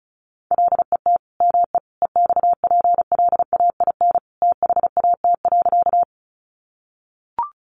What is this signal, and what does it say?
Morse code, heard as a keyed tone, about 34 wpm, 700 Hz. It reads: LET ME EXPLAIN THAT.
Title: Sentences from Top 1100 Words 35wpm